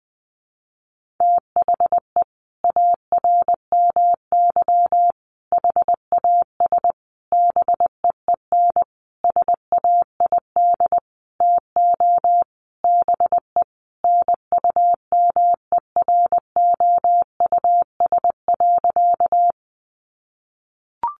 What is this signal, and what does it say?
Morse code, heard as a keyed tone, about 20 wpm, 700 Hz. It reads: THE ARMY HAS BEEN SAID TO BE NUMEROUS.
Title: Sentences from Top 1400 Words 20wpm